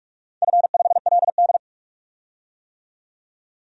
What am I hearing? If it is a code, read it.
Morse code, heard as a keyed tone, about 45 wpm, 700 Hz. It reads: F5LD
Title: DXCC-100 Call Signs - 4 Characters - 45wpm